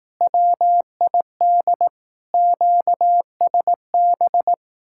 Morse code, heard as a keyed tone, about 18 wpm, 700 Hz. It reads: WID QSB
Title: Speed-Racing Common QSO Phrases Encore - 15wpm